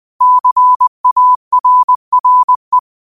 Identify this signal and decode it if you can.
Morse code, heard as a keyed tone, about 20 wpm, 1000 Hz. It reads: CARRE
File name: qotdfiles / 20wpm